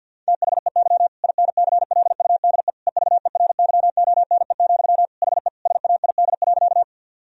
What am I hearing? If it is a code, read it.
Morse code, heard as a keyed tone, about 50 wpm, 700 Hz. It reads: THEY INCLUDE EVERYONE, HE SAID.
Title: Sentences from Top 1000 Words - Review Part 3 - 50wpm